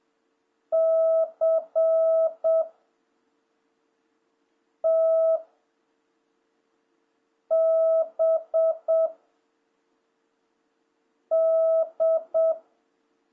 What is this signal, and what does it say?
Morse code, heard as a keyed tone, about 7 wpm, 660 Hz. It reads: C T B D
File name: jh8cbh / 2026cw